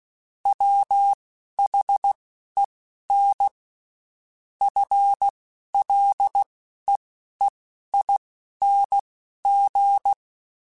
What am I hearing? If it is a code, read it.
Morse code, heard as a keyed tone, about 16 wpm, spaced as for 11 wpm, 800 Hz. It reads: WHEN FLEEING
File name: Spaced / Chuckles Spaced-mp3s / Temptation sp @15